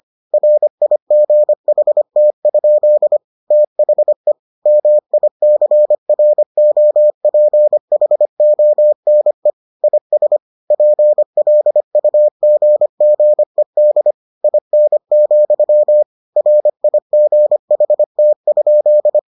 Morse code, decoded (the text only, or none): RIGHT? THE MICROPHONE IS PLUGGED IN, RIGHT?